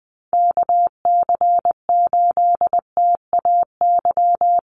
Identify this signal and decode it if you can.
Morse code, heard as a keyed tone, about 20 wpm, 700 Hz. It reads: KC8TAY
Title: USA Call Signs Repeated 2x 20wpm